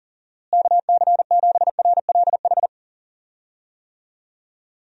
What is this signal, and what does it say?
Morse code, heard as a keyed tone, about 40 wpm, 700 Hz. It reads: KC7RLH
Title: USA Call Signs - 6 Characters - Repeated 3x - 40wpm